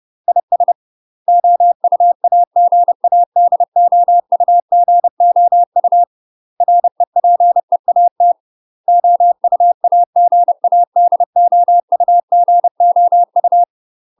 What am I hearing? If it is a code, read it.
Morse code, heard as a keyed tone, about 30 wpm, 700 Hz. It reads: IS OUAGADOUGOU REPEAT OUAGADOUGOU
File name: UK MASTER QSOs TRAINING / 30WPM / UK MASTER QSOs 0023